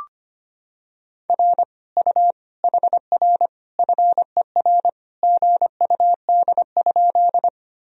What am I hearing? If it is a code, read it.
Morse code, heard as a keyed tone, about 25 wpm, 700 Hz. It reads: R U HR FER GUD?
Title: Sentences from Top 200 Words and Abbreviations 25wpm